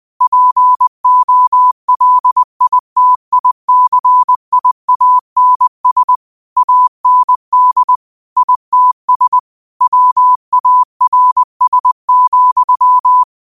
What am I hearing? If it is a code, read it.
Morse code, heard as a keyed tone, about 20 wpm, 1000 Hz. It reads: POLITICIANS AND ITS WARS,
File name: qotdfiles / 20wpm